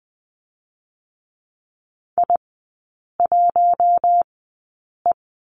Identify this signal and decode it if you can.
Morse code, heard as a keyed tone, about 20 wpm, spaced as for 8 wpm, 700 Hz. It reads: I1E
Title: 127 Lesson 21e - Number 6 - Three Characters 20wpm